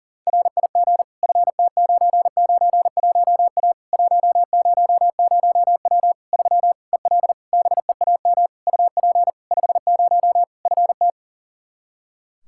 Morse code, heard as a keyed tone, about 40 wpm, 700 Hz. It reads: RIG FT991A 100W 3 EL BEAM UP 50 FT
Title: K1USN SST QSO Practice 40wpm